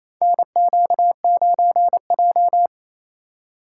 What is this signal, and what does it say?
Morse code, heard as a keyed tone, about 28 wpm, 700 Hz. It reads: NQ9J
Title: USA Call Signs - 6x Character Spacing - 28wpm